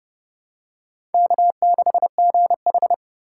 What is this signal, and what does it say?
Morse code, heard as a keyed tone, about 30 wpm, 700 Hz. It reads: K6GH
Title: USA Call Signs - 4 Characters - 8x Character Spacing - 30wpm